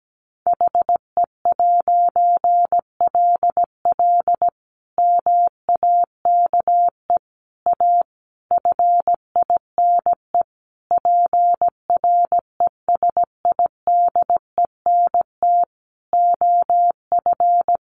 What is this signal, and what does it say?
Morse code, heard as a keyed tone, about 17 wpm, 700 Hz. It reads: HE'LL MAKE A FINE PRESIDENT OF
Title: A Sound Of Thunder 17wpm